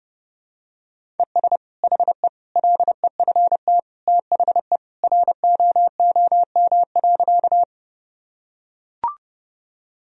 Morse code, heard as a keyed tone, about 30 wpm, 700 Hz. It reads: ES HE LEFT THE ROOM.
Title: Sentences from Top 300 Words and Abbreviations 30wpm